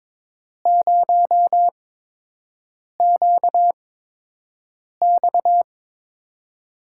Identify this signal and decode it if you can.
Morse code, heard as a keyed tone, about 22 wpm, spaced as for 6 wpm, 700 Hz. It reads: 0QX
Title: USA Call Signs - 8x Character Spacing - 22wpm